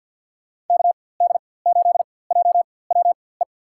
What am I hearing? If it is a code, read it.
Morse code, heard as a keyed tone, about 48 wpm, spaced as for 23 wpm, 700 Hz. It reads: KD8JWE
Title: USA Call Signs - 4x Character Spacing - 50wpm